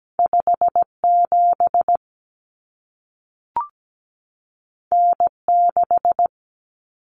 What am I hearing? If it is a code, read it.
Morse code, heard as a keyed tone, about 17 wpm, 700 Hz. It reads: 57 N6
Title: Common Cut Numbers - 2-Digit 17wpm